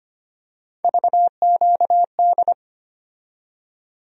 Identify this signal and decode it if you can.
Morse code, heard as a keyed tone, about 25 wpm, 700 Hz. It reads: VQD
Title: USA Call Sign Postfixes - 3 Letters 25wpm